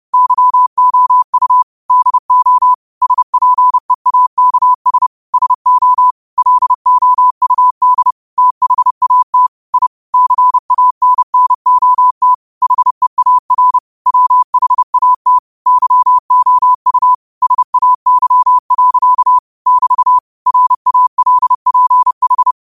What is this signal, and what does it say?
Morse code, heard as a keyed tone, about 30 wpm, 1000 Hz. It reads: YOU DO SPEAKS SO LOUD THAT I CANNOT HEAR WHAT YOU SAY. <BT> RALPH
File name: 30wpm